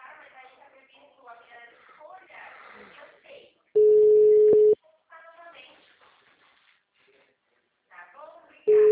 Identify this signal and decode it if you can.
Ringback tone (Europe, 425 Hz), with speech.
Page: Itu